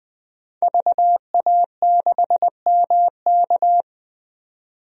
Morse code, heard as a keyed tone, about 20 wpm, 700 Hz. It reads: VA6MK
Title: DXCC-100 Call Signs - 5 Characters - 20wpm